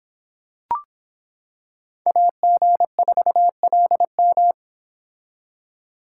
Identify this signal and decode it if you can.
Morse code, heard as a keyed tone, about 26 wpm, 700 Hz. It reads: AG4LM